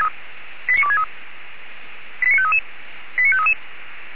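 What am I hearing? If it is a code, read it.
Five-tone selective call (ZVEI), heard as quick runs of five tones, tones of about 70 ms. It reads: 89263 89533 89533
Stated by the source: Received on an SDR, ZVEI five-tone selective call.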